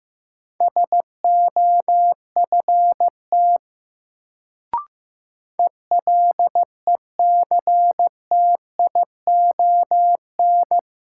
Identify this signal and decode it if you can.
Morse code, heard as a keyed tone, about 15 wpm, 700 Hz. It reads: SOFT ELECTION
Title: Sentences from Top 1500 Words - New Word Drill - 15wpm